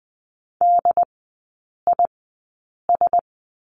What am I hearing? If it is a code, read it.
Morse code, heard as a keyed tone, about 20 wpm, spaced as for 8 wpm, 690 Hz. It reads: DIS
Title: 081 Lesson 14e - Number 2 - Three Characters 20wpm